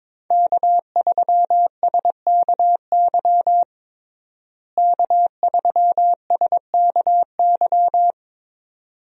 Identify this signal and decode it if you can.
Morse code, heard as a keyed tone, about 22 wpm, 700 Hz. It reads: K3SKY K3SKY
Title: USA Call Signs - 5 Characters - Repeated 2x - 22wpm